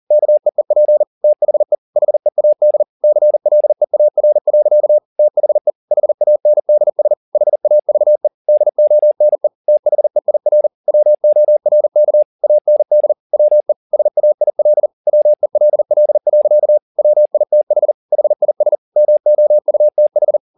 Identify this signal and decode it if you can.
Morse code, heard as a keyed tone, about 40 wpm, 600 Hz. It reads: KEEP THE HEAD CLEAR. THE HANDS HAVE DONE THEIR WORK AND WE SAIL WELL. WITH HIS MOUTH
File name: the old man and the sea 2 40wpm